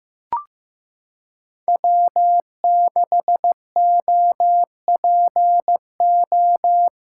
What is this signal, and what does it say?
Morse code, heard as a keyed tone, about 15 wpm, 700 Hz. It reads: W6OPO
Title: K1USN SST QSO Practice - 15wpm